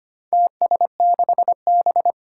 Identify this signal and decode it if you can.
Morse code, heard as a keyed tone, about 25 wpm, 700 Hz. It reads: TS6B
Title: Four Challenging Letter-Number Combinations 25wpm